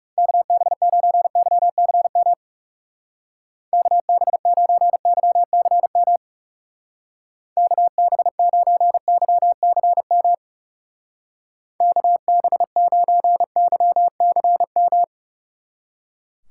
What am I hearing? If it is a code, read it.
Morse code, heard as a keyed tone, about 38 wpm, 700 Hz. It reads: KB9YCM KB9YCM KB9YCM KB9YCM